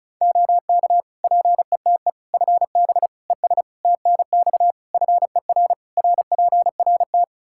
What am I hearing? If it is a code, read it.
Morse code, heard as a keyed tone, about 34 wpm, 700 Hz. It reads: OK PETE FB ES TNX FER RPRT
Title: Practice QSO - Exchange 2 - Info - 35wpm